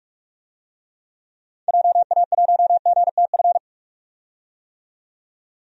Morse code, heard as a keyed tone, about 44 wpm, 700 Hz. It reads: JA1GTF
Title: DXCC-100 Call Signs - 6 Characters - 45wpm